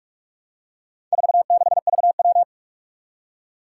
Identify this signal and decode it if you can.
Morse code, heard as a keyed tone, about 44 wpm, 700 Hz. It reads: 46VW
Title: Four Challenging Letter-Number Combinations 45wpm